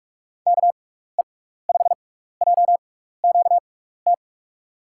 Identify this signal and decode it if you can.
Morse code, heard as a keyed tone, about 44 wpm, spaced as for 15 wpm, 700 Hz. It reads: KE5JQT
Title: USA Call Signs - 6 Characters - 6x Character Spacing - 45wpm